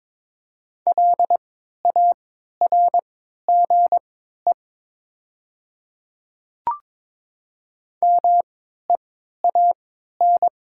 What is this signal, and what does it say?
Morse code, heard as a keyed tone, about 22 wpm, spaced as for 13 wpm, 700 Hz. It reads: LARGE MEAN
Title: 3x Character Spacing - Top 500 Words - 22wpm